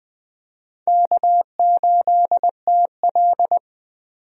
Morse code, heard as a keyed tone, about 20 wpm, 700 Hz. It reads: K8TL
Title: USA Call Signs - 4 Characters - 20wpm